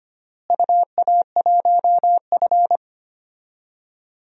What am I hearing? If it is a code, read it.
Morse code, heard as a keyed tone, about 25 wpm, 700 Hz. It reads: UA1F
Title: DXCC-100 Call Signs - 4 Characters - Repeated 2x - 25wpm